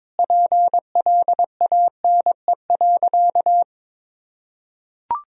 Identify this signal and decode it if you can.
Morse code, heard as a keyed tone, about 21 wpm, 700 Hz. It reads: PLANE.
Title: Sentences from Top 1000 Words - Review Part 3 - 22wpm